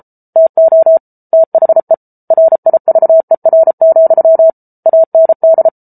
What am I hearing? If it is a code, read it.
Morse code, heard as a keyed tone, about 34 wpm, 650 Hz. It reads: TO THE RIVER, AND